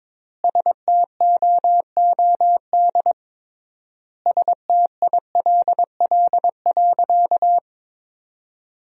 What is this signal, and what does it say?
Morse code, heard as a keyed tone, about 21 wpm, 700 Hz. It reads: STOOD STILL.
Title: 3x Word-Spacing - Sentences from Top 500 Words - 22wpm